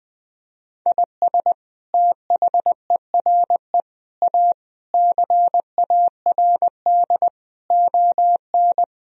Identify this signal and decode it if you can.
Morse code, heard as a keyed tone, about 21 wpm, 700 Hz. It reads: IS THERE A CARD ON
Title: Sentences from Top 1000 Words - Review Part 4 - 20wpm